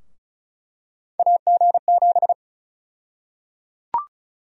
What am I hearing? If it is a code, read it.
Morse code, heard as a keyed tone, about 34 wpm, 700 Hz. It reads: AG7
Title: USA Call Sign Prefixes - Repeated 2x - 35wpm